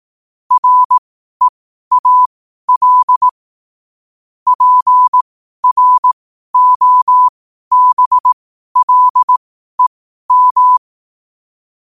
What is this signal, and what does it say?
Morse code, heard as a keyed tone, about 18 wpm, spaced as for 12 wpm, 1000 Hz. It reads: REAL PROBLEM
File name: qotdfiles / 12wpm